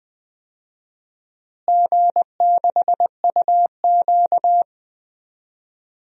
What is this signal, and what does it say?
Morse code, heard as a keyed tone, about 21 wpm, 700 Hz. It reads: G6UQ
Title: DXCC-100 Call Signs - 4 Characters - Repeated 3x - 20wpm